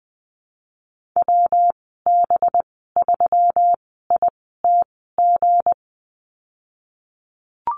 Morse code, heard as a keyed tone, about 20 wpm, spaced as for 15 wpm, 700 Hz. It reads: WB3ITG
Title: 231 Lesson 37g - Letter x - Call Signs 20wpm